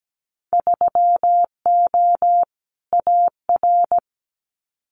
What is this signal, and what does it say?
Morse code, heard as a keyed tone, about 17 wpm, 700 Hz. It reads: 3O AR
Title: Winter Field Day Exchanges Encore 17wpm